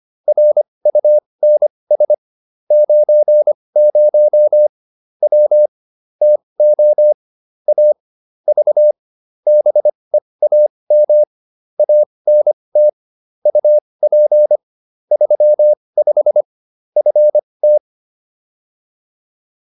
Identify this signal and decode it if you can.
Morse code, heard as a keyed tone, about 25 wpm, spaced as for 20 wpm, 600 Hz. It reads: RUNS 90 W TO A V BEAM ANT UP 35 FT
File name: QSO309 20